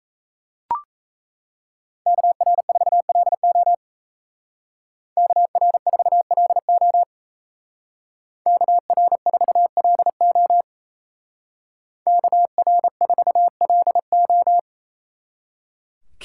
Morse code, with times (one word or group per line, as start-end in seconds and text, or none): 2.1-3.7 KR4LO
5.2-7.0 KR4LO
8.5-10.6 KR4LO
12.1-14.6 KR4LO